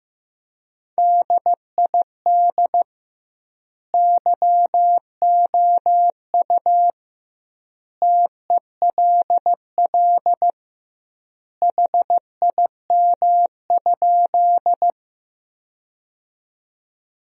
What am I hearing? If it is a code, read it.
Morse code, heard as a keyed tone, about 15 wpm, 700 Hz. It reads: DID YOU TELL HIM?
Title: 2x Word-Spacing - Sentences from Top 200 Words - 15wpm